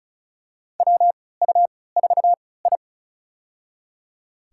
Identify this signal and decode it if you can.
Morse code, heard as a keyed tone, about 36 wpm, spaced as for 20 wpm, 700 Hz. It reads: WU4I
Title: USA Call Signs - 4 Characters - 3x Character Spacing - 35wpm